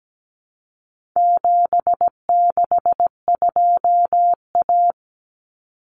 Morse code, heard as a keyed tone, about 17 wpm, 700 Hz. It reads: 762A
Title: Common Cut Numbers - 4-Digit 17wpm